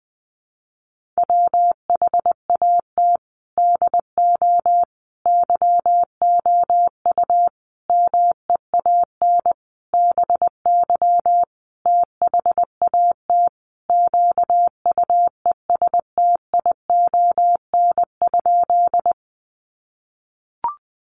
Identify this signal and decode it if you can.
Morse code, heard as a keyed tone, about 20 wpm, 700 Hz. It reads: WHAT DO YOU MEAN BY THAT QUESTION?
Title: Sentences from Top 400 Words 20wpm